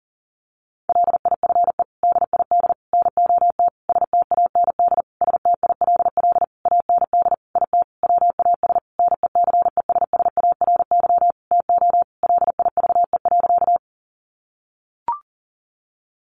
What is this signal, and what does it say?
Morse code, heard as a keyed tone, about 40 wpm, 700 Hz. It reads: LIFE DID NOT STAND STILL AND IT WAS NECESSARY TO LIVE.
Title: Sentences from Top 800 Words 40wpm